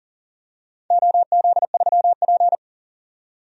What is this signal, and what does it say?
Morse code, heard as a keyed tone, about 40 wpm, 700 Hz. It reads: OZ3P
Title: DXCC-100 Call Signs - 4 Characters - 40wpm